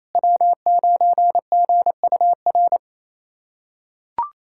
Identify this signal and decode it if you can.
Morse code, heard as a keyed tone, about 28 wpm, 700 Hz. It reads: W9GUR
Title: USA Callsign DE Callsign - 5 Characters - 28wpm